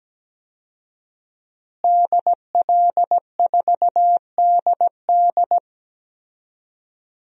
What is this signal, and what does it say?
Morse code, heard as a keyed tone, about 17 wpm, 700 Hz. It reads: DL4DD
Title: DXCC-100 Call Signs - 5 Characters - Repeated 2x - 17wpm